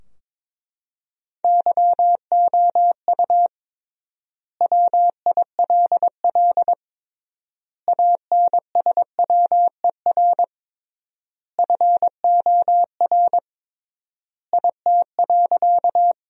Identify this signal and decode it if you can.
Morse code, heard as a keyed tone, about 22 wpm, 700 Hz. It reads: YOU WILL ANSWER FOR IT.
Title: 3x Word-Spacing - Sentences from Top 300 Words - 22wpm